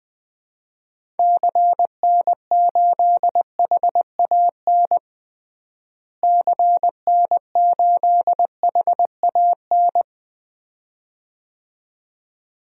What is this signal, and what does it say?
Morse code, heard as a keyed tone, about 21 wpm, 700 Hz. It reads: CN8HAN CN8HAN
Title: DXCC-100 Call Signs - 6 Characters - Repeated 2x - 20wpm